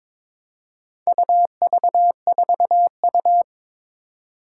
Morse code, heard as a keyed tone, about 22 wpm, 700 Hz. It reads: UV4U
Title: Four Challenging Letter-Number Combinations 22wpm